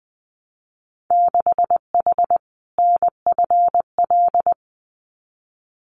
Morse code, heard as a keyed tone, about 20 wpm, 700 Hz. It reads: 6H NFL
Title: Winter Field Day Exchanges Encore 20wpm